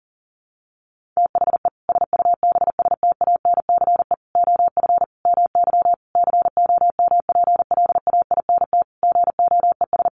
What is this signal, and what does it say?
Morse code, heard as a keyed tone, about 40 wpm, 700 Hz. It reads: THE SUBSTANCE OF MY COMPLAINT GOES
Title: Sentences from Top 1500 Words 40wpm